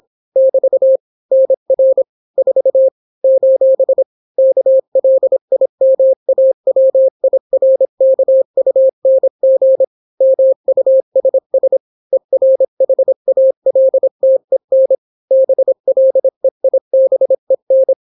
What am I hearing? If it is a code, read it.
Morse code, heard as a keyed tone, about 26 wpm, 530 Hz. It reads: <BT> NR 4 : KLIMAWIRKUNG MUSS ERHALTEN BLEIBEN